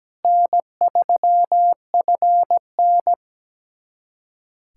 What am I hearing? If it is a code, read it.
Morse code, heard as a keyed tone, about 17 wpm, 700 Hz. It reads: N3FN
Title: USA Callsign DE Callsign - 4 Characters - 17wpm